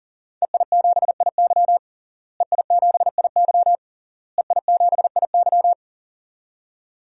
Morse code, heard as a keyed tone, about 40 wpm, 700 Hz. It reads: EI7IY EI7IY EI7IY
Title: DXCC-100 Call Signs - 5 Characters - Repeated 3x - 40wpm